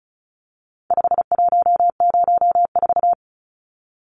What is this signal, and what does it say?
Morse code, heard as a keyed tone, about 34 wpm, 700 Hz. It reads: 5104
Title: Numbers - 4-Digit 35wpm